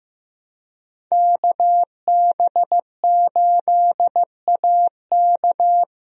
Morse code, heard as a keyed tone, about 15 wpm, 700 Hz. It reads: KB8AK
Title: USA Call Signs - 5 Characters - Repeated 3x - 15wpm